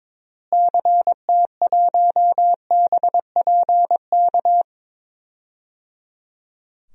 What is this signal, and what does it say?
Morse code, heard as a keyed tone, about 22 wpm, 700 Hz. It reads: CT1BPK
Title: DXCC-100 Call Signs - 6 Characters - Repeated 3x - 22wpm